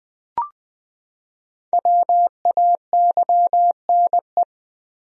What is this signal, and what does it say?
Morse code, heard as a keyed tone, about 20 wpm, 700 Hz. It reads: WAYNE
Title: USA Common Names - Max 5 Letters - 20wpm